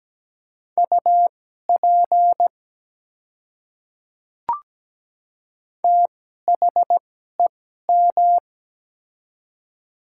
Morse code, heard as a keyed tone, about 17 wpm, spaced as for 12 wpm, 700 Hz. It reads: UP THEM
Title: 2x Character Spacing - Top 200 Words - 17wpm